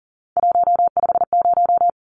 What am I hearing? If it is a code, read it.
Morse code, heard as a keyed tone, about 40 wpm, 700 Hz. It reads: 150
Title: Numbers - 3-Digit 40wpm